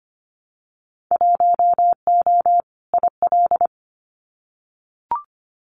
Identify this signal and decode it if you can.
Morse code, heard as a keyed tone, about 25 wpm, 700 Hz. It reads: 1O IL
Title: Winter Field Day Exchanges 25wpm